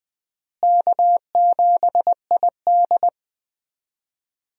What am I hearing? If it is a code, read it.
Morse code, heard as a keyed tone, about 21 wpm, 700 Hz. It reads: K7ID
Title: USA Call Signs - 4 Characters - Repeated 2x - 20wpm